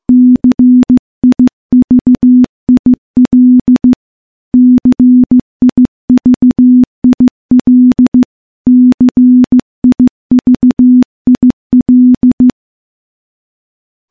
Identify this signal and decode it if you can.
Morse code, heard as a keyed tone, about 15 wpm, 250 Hz. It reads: CIVIL CIVIL CIVIL
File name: morse / morse-training-15WPM